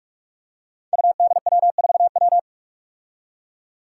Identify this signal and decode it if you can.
Morse code, heard as a keyed tone, about 44 wpm, 700 Hz. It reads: UDW4W
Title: Five Challenging Letter-Number Combinations 45wpm